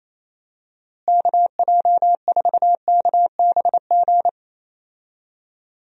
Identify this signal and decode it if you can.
Morse code, heard as a keyed tone, about 28 wpm, 700 Hz. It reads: KJ4KBG
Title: USA Call Signs - 6 Characters - Repeated 2x - 28wpm